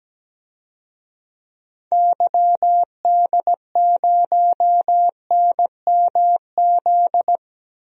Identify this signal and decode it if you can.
Morse code, heard as a keyed tone, about 17 wpm, 700 Hz. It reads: YD0NMZ
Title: DXCC-100 Call Signs - 6 Characters - 17wpm